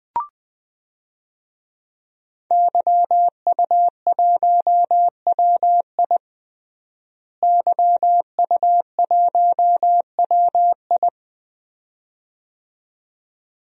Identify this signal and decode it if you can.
Morse code, heard as a keyed tone, about 20 wpm, 700 Hz. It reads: YU1WI YU1WI
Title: DXCC-100 Call Signs - 5 Characters - Repeated 2x - 20wpm